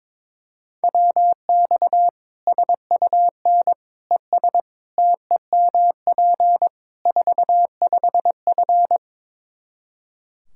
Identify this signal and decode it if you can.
Morse code, heard as a keyed tone, about 22 wpm, 700 Hz. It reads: WX SUN ES TEMP 45F
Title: Practice QSO - Exchange 2 - Info - 22wpm